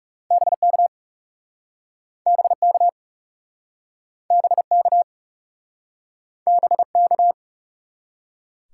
Morse code, heard as a keyed tone, about 38 wpm, 700 Hz. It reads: BK BK BK BK